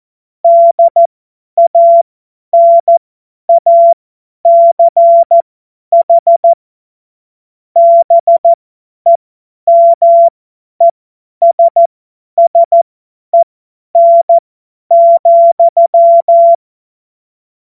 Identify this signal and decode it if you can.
Morse code, heard as a keyed tone, about 14 wpm, spaced as for 10 wpm, 680 Hz. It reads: DANACH BEMESSEN,
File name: Witz-16-September-2025-50-F70